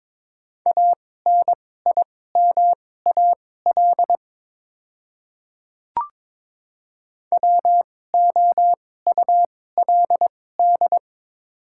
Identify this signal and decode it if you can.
Morse code, heard as a keyed tone, about 22 wpm, spaced as for 16 wpm, 700 Hz. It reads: ANIMAL WOULD
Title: 2x Character Spacing - Top 500 Words - 22wpm